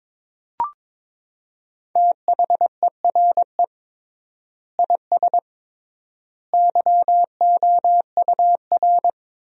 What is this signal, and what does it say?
Morse code, heard as a keyed tone, about 22 wpm, 700 Hz. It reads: THERE IS YOUR
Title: 3x Word-Spacing - Sentences from Top 500 Words - 22wpm